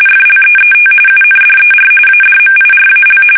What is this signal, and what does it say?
RTTY, about 45.45 baud. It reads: FLEET UNDER ADMIRAL
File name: fsk850 / battleshipnj